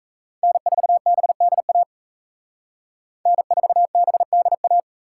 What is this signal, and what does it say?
Morse code, heard as a keyed tone, about 36 wpm, 700 Hz. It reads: N4BDA N4BDA